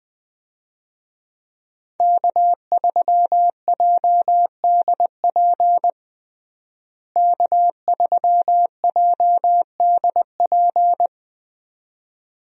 Morse code, heard as a keyed tone, about 20 wpm, 700 Hz. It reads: K3JDP K3JDP
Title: USA Call Signs - 5 Characters - Repeated 2x - 20wpm